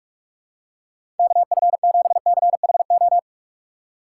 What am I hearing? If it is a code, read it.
Morse code, heard as a keyed tone, about 43 wpm, 700 Hz. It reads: KF7CHO
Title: USA Call Signs - 6 Characters - 4x Character Spacing - 45wpm